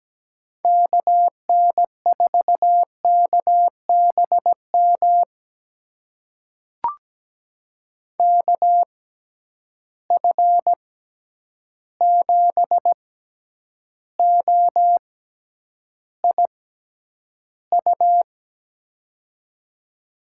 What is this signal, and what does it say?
Morse code, heard as a keyed tone, about 17 wpm, spaced as for 9 wpm, 700 Hz. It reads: KN4KBM KF7OIU
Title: USA Call Signs - 6 Characters - 6x Character Spacing - 17wpm